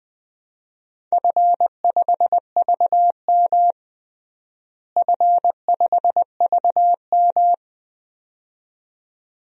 Morse code, heard as a keyed tone, about 20 wpm, 700 Hz. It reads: F5VM F5VM
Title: DXCC-100 Call Signs - 4 Characters - Repeated 2x - 20wpm